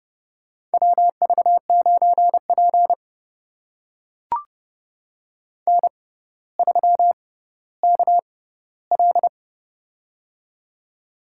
Morse code, heard as a keyed tone, about 28 wpm, spaced as for 15 wpm, 700 Hz. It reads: WV9P N3KL
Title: USA Call Signs - 4 Characters - 6x Character Spacing - 30wpm